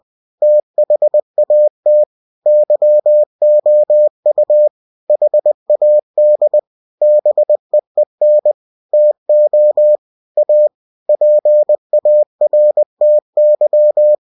Morse code, heard as a keyed tone, about 20 wpm, 600 Hz. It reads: THAT YOU HAD BEEN TO A PARTY
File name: cw-books / winnie-the-pooh / winnie-20wpm-page-0005